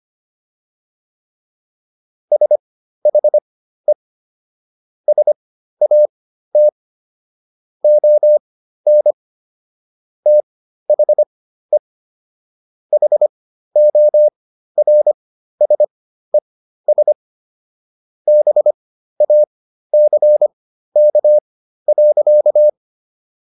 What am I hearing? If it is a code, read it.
Morse code, heard as a keyed tone, about 25 wpm, spaced as for 13 wpm, 600 Hz. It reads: SHE SAT ON THE HORSES BACK.